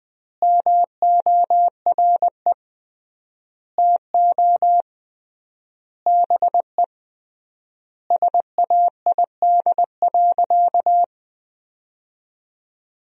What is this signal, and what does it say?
Morse code, heard as a keyed tone, about 20 wpm, 700 Hz. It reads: MORE TO BE SAID.
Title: 3x Word-Spacing - Sentences from Top 500 Words - 20wpm